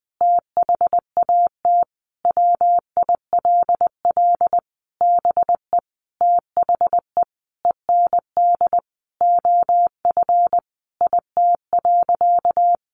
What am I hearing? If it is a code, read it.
Morse code, heard as a keyed tone, about 20 wpm, 700 Hz. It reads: THAT WILL BE THE END OF IT.
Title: Sentences from Top 1200 Words 20wpm